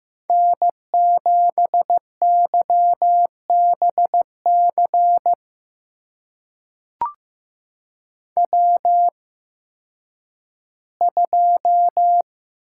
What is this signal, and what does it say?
Morse code, heard as a keyed tone, about 15 wpm, spaced as for 10 wpm, 700 Hz. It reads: N7YBC W2
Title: USA Call Signs - 5 Characters - 8x Character Spacing - 15wpm